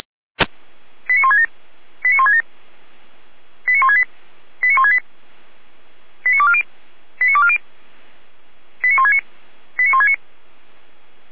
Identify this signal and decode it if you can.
Five-tone selective call (ZVEI), heard as quick runs of five tones, tones of about 70 ms. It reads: 89167 89167 89168 89168 89240 89240 89169 89169